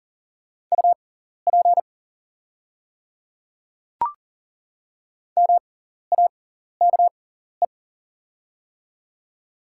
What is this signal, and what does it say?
Morse code, heard as a keyed tone, about 40 wpm, spaced as for 14 wpm, 700 Hz. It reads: UP MAKE
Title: 6x Character Spacing - Top 200 Words - 40wpm